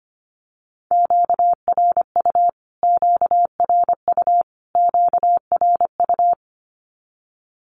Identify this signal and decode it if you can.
Morse code, heard as a keyed tone, about 25 wpm, 700 Hz. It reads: QRU QRU QRU
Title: Q-Codes Repeated 3x 25wpm